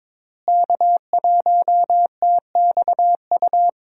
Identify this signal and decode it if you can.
Morse code, heard as a keyed tone, about 22 wpm, 700 Hz. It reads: K1TXU
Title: USA Call Signs - 5 Characters - Repeated 2x - 22wpm